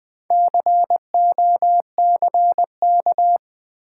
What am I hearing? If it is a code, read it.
Morse code, heard as a keyed tone, about 20 wpm, 700 Hz. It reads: COCK
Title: Build A Word - 4 Letters At A Time - 8-Letter Words - 20wpm